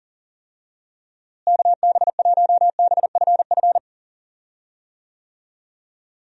Morse code, heard as a keyed tone, about 40 wpm, 700 Hz. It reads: KB1BFF
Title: USA Call Signs - 6 Characters - Repeated 3x - 40wpm